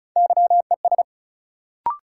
Morse code, heard as a keyed tone, about 36 wpm, 700 Hz. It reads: YES
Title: Sentences from Top 500 Words - New Word Drill - 35wpm